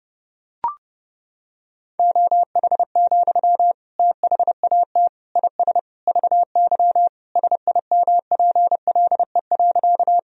Morse code, heard as a keyed tone, about 30 wpm, 700 Hz. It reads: OH, THAT IS VY SIMPLE.
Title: Sentences from Top 500 Words and Abbreviations 30wpm